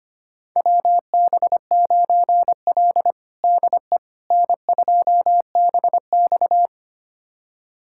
Morse code, heard as a keyed tone, about 25 wpm, 700 Hz. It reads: WB9L DE N2BX
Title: USA Callsign DE Callsign - 4 Characters - 25wpm